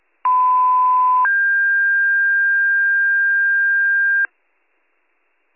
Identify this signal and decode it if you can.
Two-tone page, heard as a one-second tone, then a three-second tone, about 1010 then 1600 Hz, with noise. It is heard from 0.2 to 4.3 s.